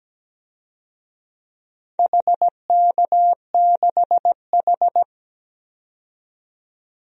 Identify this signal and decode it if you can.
Morse code, heard as a keyed tone, about 17 wpm, 700 Hz. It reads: HK6H
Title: DXCC-100 Call Signs - 4 Characters - 17wpm